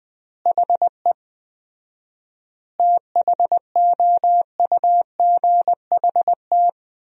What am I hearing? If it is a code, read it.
Morse code, heard as a keyed tone, about 19 wpm, 700 Hz. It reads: HE THOUGHT
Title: 4x Word-Spacing - Sentences from Top 300 Words - 20wpm